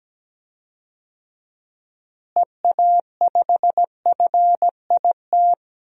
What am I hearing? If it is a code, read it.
Morse code, heard as a keyed tone, about 17 wpm, 700 Hz. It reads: EA5FIT
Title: DXCC-100 Call Signs - 6 Characters - Repeated 3x - 17wpm